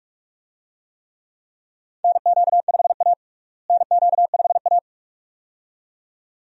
Morse code, heard as a keyed tone, about 42 wpm, 700 Hz. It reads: NQ5A NQ5A
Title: USA Call Signs - 4 Characters - Repeated 2x - 45wpm